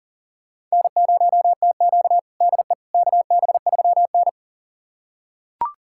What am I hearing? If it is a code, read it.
Morse code, heard as a keyed tone, about 40 wpm, 700 Hz. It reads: N0TQ DE KB3N
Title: USA Callsign DE Callsign - 4 Characters - 40wpm